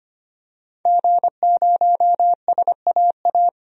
Morse code, heard as a keyed tone, about 25 wpm, 700 Hz. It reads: G0SAA
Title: DXCC-100 Call Signs - 5 Characters - Repeated 2x - 25wpm